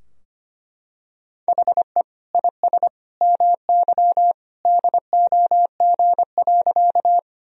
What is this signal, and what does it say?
Morse code, heard as a keyed tone, about 25 wpm, 700 Hz. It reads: HE IS MY DOG.
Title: Sentences from Top 1000 Words - Review Part 2 - 25wpm